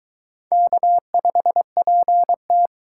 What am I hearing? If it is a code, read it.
Morse code, heard as a keyed tone, about 22 wpm, 700 Hz. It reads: K5PT